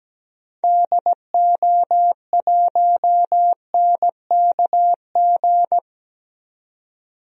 Morse code, heard as a keyed tone, about 17 wpm, 700 Hz. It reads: DO1NKG
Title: DXCC-100 Call Signs - 6 Characters - 17wpm